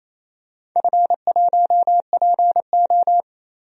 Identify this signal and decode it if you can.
Morse code, heard as a keyed tone, about 28 wpm, 700 Hz. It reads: F1PO
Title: DXCC-100 Call Signs - 4 Characters - 28wpm